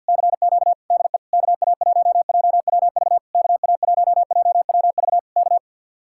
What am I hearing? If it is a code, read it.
Morse code, heard as a keyed tone, about 64 wpm, 700 Hz. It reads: CQ DE KA1JWV KA1JWV K